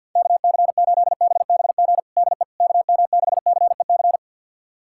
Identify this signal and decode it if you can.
Morse code, heard as a keyed tone, about 47 wpm, 700 Hz. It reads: KC8BBG DE KM6CEC
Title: USA Callsign DE Callsign 50wpm